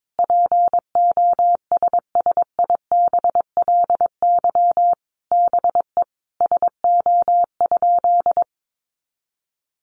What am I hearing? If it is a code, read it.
Morse code, heard as a keyed tone, about 22 wpm, 700 Hz. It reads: POSSIBLY BE SO?
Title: Sentences from Top 1300 Words 22wpm